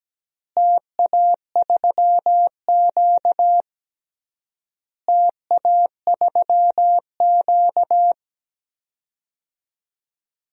Morse code, heard as a keyed tone, about 17 wpm, 700 Hz. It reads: TA3Q TA3Q
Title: DXCC-100 Call Signs - 4 Characters - Repeated 3x - 17wpm